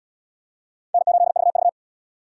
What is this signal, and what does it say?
Morse code, heard as a keyed tone, about 69 wpm, 700 Hz. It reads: N8VV